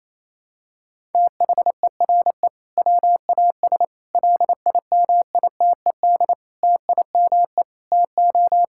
Morse code, heard as a keyed tone, about 28 wpm, 700 Hz. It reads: THERE WAS LIMITED TIME TO
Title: Sentences from Top 1000 Words - Review Part 6 - 28wpm